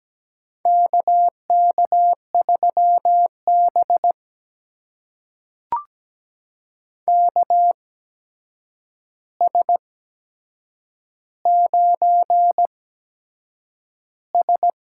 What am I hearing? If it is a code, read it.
Morse code, heard as a keyed tone, about 17 wpm, spaced as for 7 wpm, 700 Hz. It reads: KK3B KS9S
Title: USA Call Signs - 4 Characters - 8x Character Spacing - 17wpm